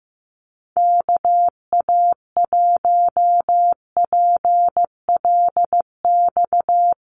Morse code, heard as a keyed tone, about 15 wpm, 700 Hz. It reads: KA1PLX
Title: USA Call Signs Repeated 3x 15wpm